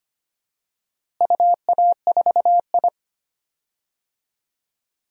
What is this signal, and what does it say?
Morse code, heard as a keyed tone, about 25 wpm, 700 Hz. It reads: UA4I
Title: DXCC-100 Call Signs - 4 Characters - Repeated 3x - 25wpm